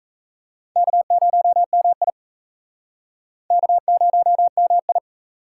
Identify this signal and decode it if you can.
Morse code, heard as a keyed tone, about 35 wpm, 700 Hz. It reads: K0MI K0MI